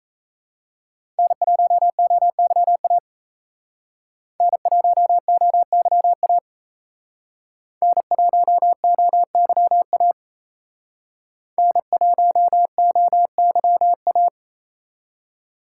Morse code, heard as a keyed tone, about 35 wpm, 700 Hz. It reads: N1OYA N1OYA N1OYA N1OYA
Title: Speed Racing - USA 1x3 Call Signs 28wpm